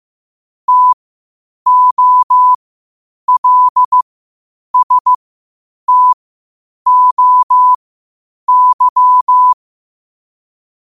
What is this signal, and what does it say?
Morse code, heard as a keyed tone, about 15 wpm, spaced as for 8 wpm, 1000 Hz. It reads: TOLSTOY